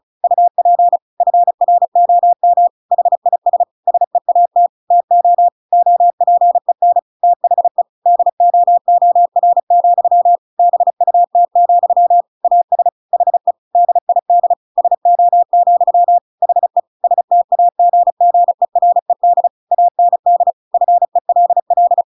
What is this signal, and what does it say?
Morse code, heard as a keyed tone, about 34 wpm, 700 Hz. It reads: UP FROM HIS SEAT TO OPEN THE DOOR, BUT, AS HE DID SO, HE STAGGERED AND FELL
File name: Pinocchio / Pinocchio ch7 35 35 ik1hge